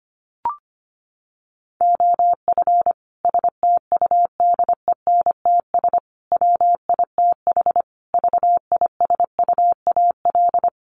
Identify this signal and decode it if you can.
Morse code, heard as a keyed tone, about 25 wpm, 700 Hz. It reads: OF STUDENTS WITH VISUAL
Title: Sets of 5 Words Encore 25wpm